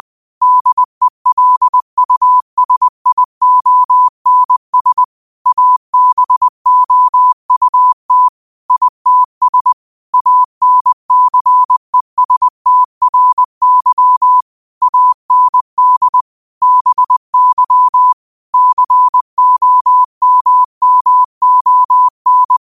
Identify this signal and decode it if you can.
Morse code, heard as a keyed tone, about 20 wpm, 1000 Hz. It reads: DELUSIONS ABOUT ITS ANCESTRY AND BY COMMON
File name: qotdfiles / 20wpm